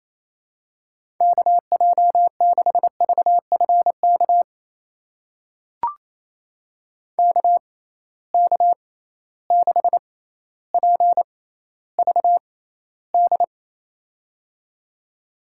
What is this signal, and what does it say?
Morse code, heard as a keyed tone, about 28 wpm, spaced as for 14 wpm, 700 Hz. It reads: KJ6VFK KK6PVD
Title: USA Call Signs - 6 Characters - 6x Character Spacing - 28wpm